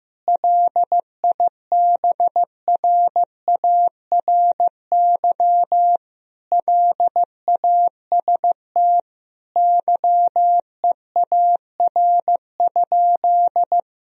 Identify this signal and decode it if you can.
Morse code, heard as a keyed tone, about 16 wpm, 700 Hz. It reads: LIBRARY LAST YEAR?
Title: Sentences from Top 1000 Words - Review Part 4 - 15wpm